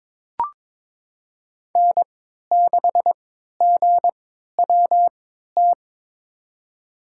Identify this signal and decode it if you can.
Morse code, heard as a keyed tone, about 22 wpm, spaced as for 12 wpm, 700 Hz. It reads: N6GWT
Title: USA Call Signs - 5 Characters - 3x Character Spacing - 22wpm